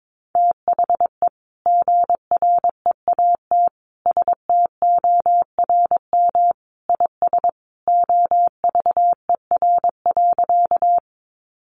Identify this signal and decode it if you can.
Morse code, heard as a keyed tone, about 22 wpm, 700 Hz. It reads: THE GREAT STORM IS OVER.
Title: Sentences from Top 1500 Words 22wpm